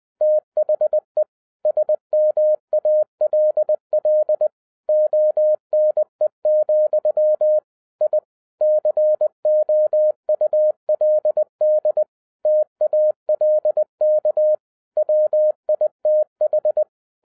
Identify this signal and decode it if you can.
Morse code, heard as a keyed tone, about 20 wpm, 610 Hz. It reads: THE SMALL ONE, I COULD TALK WITH
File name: 2012-01-mp3 / n05-CAME BACK TO HAM RADIO AFTER 30 YEARS OF INTERVAL-20wpm